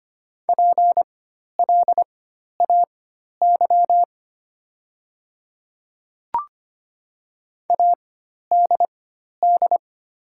Morse code, heard as a keyed tone, about 25 wpm, spaced as for 12 wpm, 700 Hz. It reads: PLAY ADD
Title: 4x Character Spacing - Top 300 Words - 25wpm